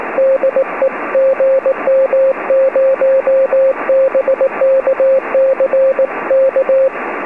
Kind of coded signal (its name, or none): Morse code